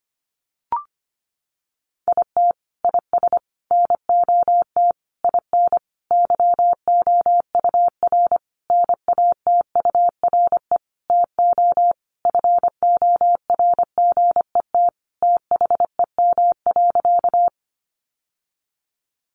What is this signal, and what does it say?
Morse code, heard as a keyed tone, about 25 wpm, 700 Hz. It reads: IT IS NOT IN YOUR NATURE TO FORGET THEM.
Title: Sentences from Top 1200 Words 25wpm